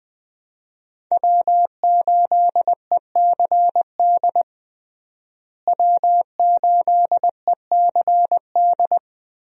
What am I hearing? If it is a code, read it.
Morse code, heard as a keyed tone, about 20 wpm, 700 Hz. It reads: W8ECD W8ECD
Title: USA Call Signs - 5 Characters - Repeated 3x - 20wpm